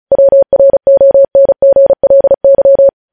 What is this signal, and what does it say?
Morse code, heard as a keyed tone, about 36 wpm, 550 Hz. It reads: WRONGLY
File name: podcast / News headlines-1x-35x35